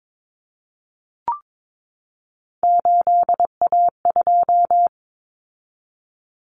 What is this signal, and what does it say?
Morse code, heard as a keyed tone, about 22 wpm, 700 Hz. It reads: 8A2
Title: Common Cut Numbers - 3-Digit 22wpm